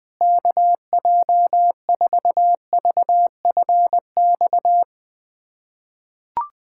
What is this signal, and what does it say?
Morse code, heard as a keyed tone, about 21 wpm, 700 Hz. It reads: KJ4VFX
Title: USA Callsign DE Callsign - 6 Characters - 20wpm